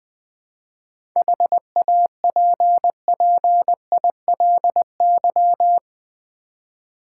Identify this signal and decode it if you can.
Morse code, heard as a keyed tone, about 21 wpm, 700 Hz. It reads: HAPPILY
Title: 7 Letter Words - First 3 Letters Given - 20wpm